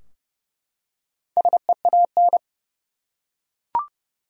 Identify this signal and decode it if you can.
Morse code, heard as a keyed tone, about 30 wpm, 700 Hz. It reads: SEAN
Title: USA Common Names - Max 5 Letters - 30wpm